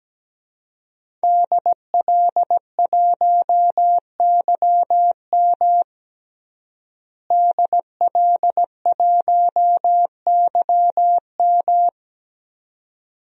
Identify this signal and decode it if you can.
Morse code, heard as a keyed tone, about 17 wpm, 700 Hz. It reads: DL1YM DL1YM
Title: DXCC-100 Call Signs - 5 Characters - Repeated 2x - 17wpm